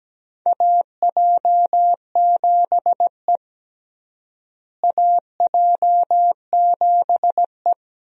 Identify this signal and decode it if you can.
Morse code, heard as a keyed tone, about 17 wpm, 700 Hz. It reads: AJ7E AJ7E
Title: USA Call Signs - 4 Characters - Repeated 2x - 17wpm